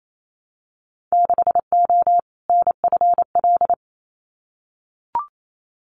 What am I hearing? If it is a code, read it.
Morse code, heard as a keyed tone, about 28 wpm, 700 Hz. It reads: 6O NFL
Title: Winter Field Day Exchanges 28wpm